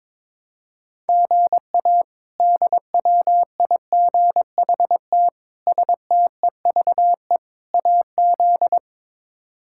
Morse code, heard as a keyed tone, about 22 wpm, 700 Hz. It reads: GA DWIGHT STEVE AZ
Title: K1USN SST QSO Practice - 22wpm